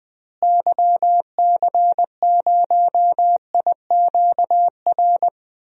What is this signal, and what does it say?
Morse code, heard as a keyed tone, about 20 wpm, 700 Hz. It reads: YC0IQR
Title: DXCC-100 Call Signs - 6 Characters - Repeated 2x - 20wpm